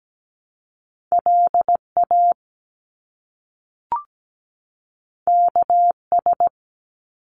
Morse code, heard as a keyed tone, about 17 wpm, 700 Hz. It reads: LA KS
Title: US States 17wpm